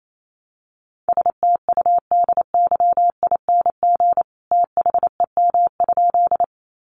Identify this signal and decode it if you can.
Morse code, heard as a keyed tone, about 28 wpm, 700 Hz. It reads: STUDYING THEM?
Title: A Sound Of Thunder 28wpm